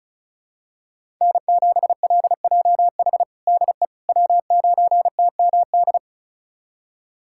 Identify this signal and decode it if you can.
Morse code, heard as a keyed tone, about 36 wpm, 700 Hz. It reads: N7LJH DE W9TMD
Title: USA Callsign DE Callsign - 5 Characters - 35wpm